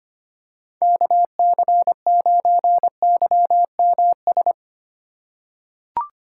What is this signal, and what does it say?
Morse code, heard as a keyed tone, about 25 wpm, 700 Hz. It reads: KC9YMS
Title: USA Call Signs - 2x Character Spacing - 25wpm